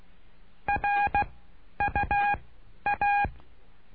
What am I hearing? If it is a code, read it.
Morse code, heard as a keyed tone, about 16 wpm, spaced as for 10 wpm, 820 Hz. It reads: RUA
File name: cursos / CURSO DE TELEGRAFIA / Telegrafia - Trilha 09